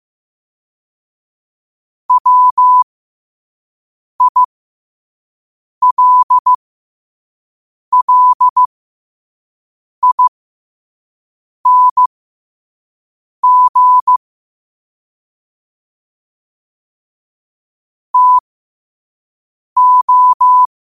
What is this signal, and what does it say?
Morse code, heard as a keyed tone, about 15 wpm, spaced as for 5 wpm, 1000 Hz. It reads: WILLING TO
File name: qotdfiles / 5wpm